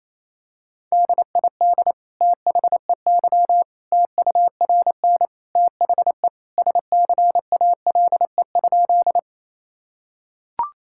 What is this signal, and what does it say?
Morse code, heard as a keyed tone, about 28 wpm, 700 Hz. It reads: DID THEY TURN THE SCALE?
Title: Sentences from Top 1000 Words - Review Part 5 - 28wpm